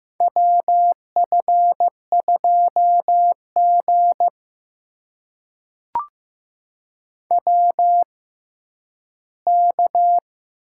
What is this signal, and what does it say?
Morse code, heard as a keyed tone, about 15 wpm, spaced as for 10 wpm, 700 Hz. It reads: WF2G WK
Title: USA Call Signs - 4 Characters - 6x Character Spacing - 15wpm